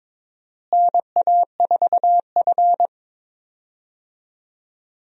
Morse code, heard as a keyed tone, about 22 wpm, 700 Hz. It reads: NA4F
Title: USA Call Signs - 4 Characters - Repeated 2x - 22wpm